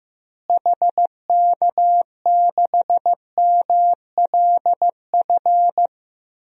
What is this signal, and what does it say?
Morse code, heard as a keyed tone, about 15 wpm, 700 Hz. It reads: HK6MLF
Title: DXCC-100 Call Signs - 6 Characters - Repeated 2x - 15wpm